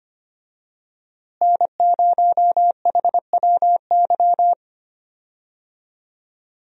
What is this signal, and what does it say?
Morse code, heard as a keyed tone, about 25 wpm, 700 Hz. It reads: N0HWY
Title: USA Call Signs - 5 Characters - Repeated 3x - 25wpm